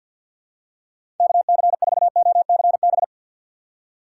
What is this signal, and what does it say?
Morse code, heard as a keyed tone, about 47 wpm, 700 Hz. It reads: KC4OCB
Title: USA Call Signs - 6x Character Spacing - 50wpm